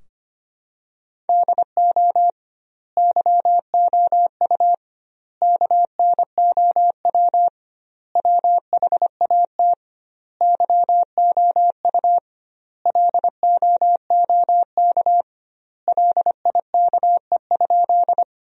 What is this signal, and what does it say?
Morse code, heard as a keyed tone, about 25 wpm, 700 Hz. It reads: DO YOU KNOW WHAT YOU LOOK LIKE?
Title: 2x Word-Spacing - Sentences from Top 100 Words - 25wpm